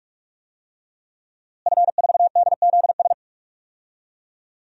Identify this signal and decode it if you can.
Morse code, heard as a keyed tone, about 43 wpm, 700 Hz. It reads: F4DZS
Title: DXCC-100 Call Signs - 5 Characters - Repeated 3x - 45wpm